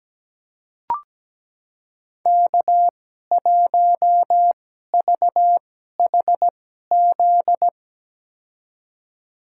Morse code, heard as a keyed tone, about 17 wpm, spaced as for 12 wpm, 700 Hz. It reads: K1VHZ
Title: USA Call Signs - 5 Characters - 2x Character Spacing - 17wpm